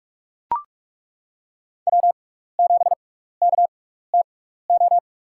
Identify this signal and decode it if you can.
Morse code, heard as a keyed tone, about 42 wpm, spaced as for 15 wpm, 700 Hz. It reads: W7KTO
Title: USA Call Signs - 6x Character Spacing - 45wpm